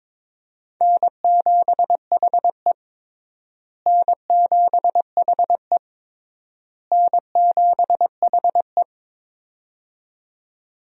Morse code, heard as a keyed tone, about 22 wpm, 700 Hz. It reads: N7HE N7HE N7HE
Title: USA Call Signs - 4 Characters - Repeated 3x - 22wpm